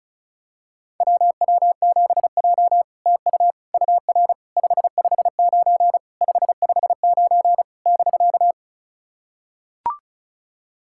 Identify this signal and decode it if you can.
Morse code, heard as a keyed tone, about 36 wpm, 700 Hz. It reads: WW7J TU UR 559 559 <BK>
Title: SOTA QSO Practice 35wpm